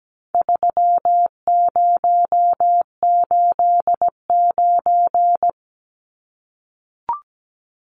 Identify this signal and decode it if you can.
Morse code, heard as a keyed tone, about 17 wpm, 700 Hz. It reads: 3089
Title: Numbers - 4-Digit 17wpm